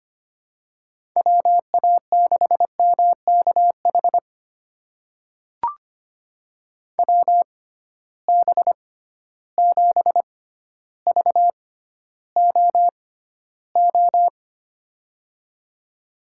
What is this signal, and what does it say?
Morse code, heard as a keyed tone, about 25 wpm, spaced as for 13 wpm, 700 Hz. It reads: WA6MKH WB7VOO
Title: USA Call Signs - 6 Characters - 6x Character Spacing - 25wpm